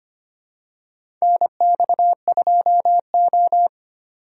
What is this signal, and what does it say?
Morse code, heard as a keyed tone, about 25 wpm, 700 Hz. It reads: NX2O
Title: USA Call Signs - 4 Characters - 4x Character Spacing - 25wpm